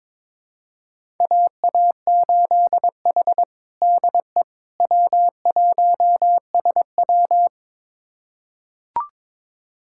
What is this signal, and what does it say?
Morse code, heard as a keyed tone, about 22 wpm, 700 Hz. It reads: AA8H DE W1SW
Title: USA Callsign DE Callsign - 4 Characters - 22wpm